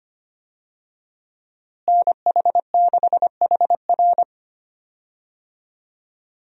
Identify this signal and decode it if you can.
Morse code, heard as a keyed tone, about 25 wpm, 700 Hz. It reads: NH6HR
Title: USA Call Signs - 5 Characters - Repeated 3x - 25wpm